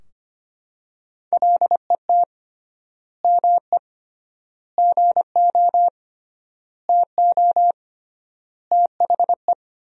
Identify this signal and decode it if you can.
Morse code, heard as a keyed tone, about 25 wpm, 700 Hz. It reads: LET ME GO TO THE
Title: 3x Word-Spacing - Sentences from Top 300 Words - 25wpm